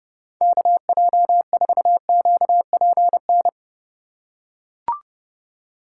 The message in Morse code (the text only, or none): KJ4QPN